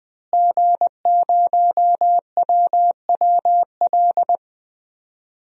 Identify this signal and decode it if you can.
Morse code, heard as a keyed tone, about 20 wpm, 700 Hz. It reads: G0WWL
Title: DXCC-100 Call Signs - 5 Characters - Repeated 3x - 20wpm